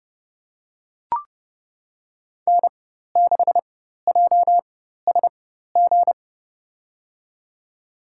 Morse code, heard as a keyed tone, about 32 wpm, spaced as for 14 wpm, 700 Hz. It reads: N6JSG
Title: USA Call Signs - 5 Characters - 4x Character Spacing - 30wpm